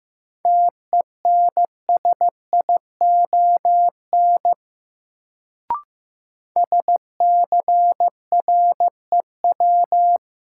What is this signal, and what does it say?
Morse code, heard as a keyed tone, about 15 wpm, 700 Hz. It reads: TENSION SCREW
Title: Sentences from Top 1600 Words - New Word Drill - 15wpm